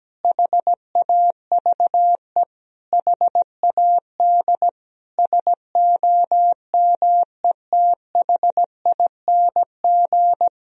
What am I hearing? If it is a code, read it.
Morse code, heard as a keyed tone, about 17 wpm, 700 Hz. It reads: HAVE HAD SOMETHING
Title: Sentences from Top 1000 Words - Review Part 2 - 17wpm